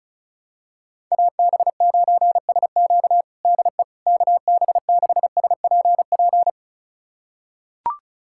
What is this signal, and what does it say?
Morse code, heard as a keyed tone, about 36 wpm, 700 Hz. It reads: AB9SQ DE KB6SPP
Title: USA Callsign DE Callsign 35wpm